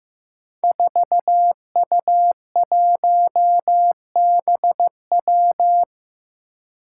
Morse code, heard as a keyed tone, about 15 wpm, 700 Hz. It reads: 4U1BW
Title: Five Challenging Letter-Number Combinations 15wpm